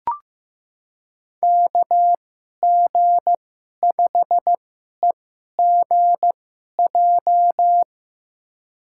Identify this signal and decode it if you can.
Morse code, heard as a keyed tone, about 15 wpm, spaced as for 11 wpm, 700 Hz. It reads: KG5EGJ
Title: USA Call Signs - 6 Characters - 2x Character Spacing - 15wpm